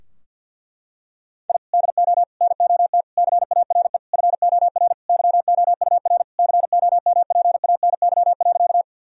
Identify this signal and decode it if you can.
Morse code, heard as a keyed tone, about 51 wpm, 700 Hz. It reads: I DO NOT CARE FOR YOUR COMPANY.
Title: Sentences from Top 1000 Words - Review Part 3 - 50wpm